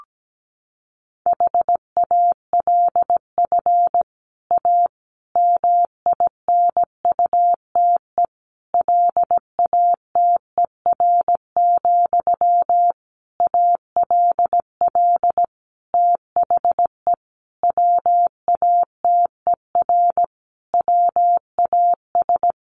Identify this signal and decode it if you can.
Morse code, heard as a keyed tone, about 17 wpm, 700 Hz. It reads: HALF A MINUTE LATER, ALL THE WATER WAS
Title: Sentences from Top 1100 Words 17wpm